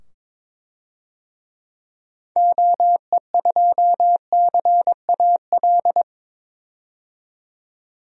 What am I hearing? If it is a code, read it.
Morse code, heard as a keyed tone, about 22 wpm, 700 Hz. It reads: OE2CAL
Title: DXCC-100 Call Signs - 6 Characters - 22wpm